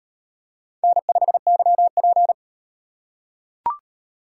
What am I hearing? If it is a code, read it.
Morse code, heard as a keyed tone, about 38 wpm, 700 Hz. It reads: N5YP